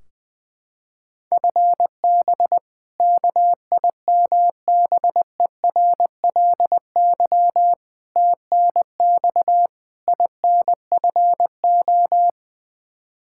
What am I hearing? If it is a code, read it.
Morse code, heard as a keyed tone, about 20 wpm, 700 Hz. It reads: FB KIMBERLY TNX INFO
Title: Practice QSO - Exchange 3 - More Info - 20wpm